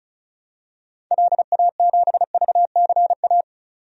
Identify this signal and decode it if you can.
Morse code, heard as a keyed tone, about 34 wpm, 700 Hz. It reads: LA7VCA
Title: DXCC-100 Call Signs - 6 Characters - 35wpm